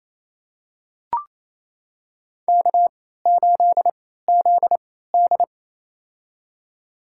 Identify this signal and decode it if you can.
Morse code, heard as a keyed tone, about 28 wpm, spaced as for 16 wpm, 700 Hz. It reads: K8ZD
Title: USA Call Signs - 4 Characters - 3x Character Spacing - 28wpm